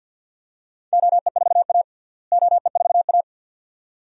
Morse code, heard as a keyed tone, about 48 wpm, 700 Hz. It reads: OE4A OE4A
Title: DXCC-100 Call Signs - 4 Characters - Repeated 2x - 50wpm